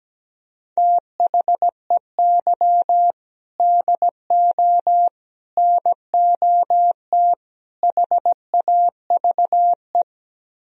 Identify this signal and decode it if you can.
Morse code, heard as a keyed tone, about 17 wpm, 700 Hz. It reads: THEY DO NOT HAVE
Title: Speed-Racing - Sets of 4 Words 17wpm